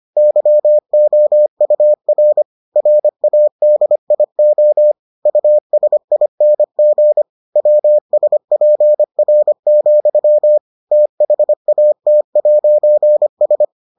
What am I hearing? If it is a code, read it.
Morse code, heard as a keyed tone, about 25 wpm, 600 Hz. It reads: YOUR RADIO USING WSPR, THAT'S